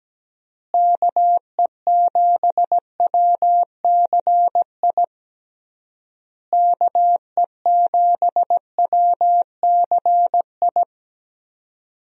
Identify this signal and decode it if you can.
Morse code, heard as a keyed tone, about 17 wpm, 700 Hz. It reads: KE7WCI KE7WCI
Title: USA Call Signs - 6 Characters - Repeated 3x - 17wpm